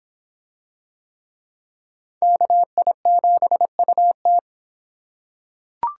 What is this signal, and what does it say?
Morse code, heard as a keyed tone, about 23 wpm, 700 Hz. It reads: KI7UT